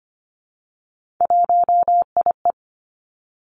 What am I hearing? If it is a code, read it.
Morse code, heard as a keyed tone, about 25 wpm, 700 Hz. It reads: 1IE
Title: Three Challenging Letter-Number Combinations 25wpm